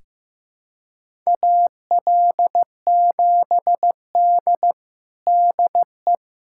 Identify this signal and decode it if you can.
Morse code, heard as a keyed tone, about 15 wpm, 700 Hz. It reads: AL7D DE
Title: USA Callsign DE Callsign - 4 Characters - 15wpm